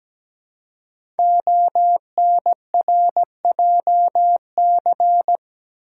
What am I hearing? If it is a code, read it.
Morse code, heard as a keyed tone, about 17 wpm, 700 Hz. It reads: ONRJC
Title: Code Groups - 5 Letters - 17wpm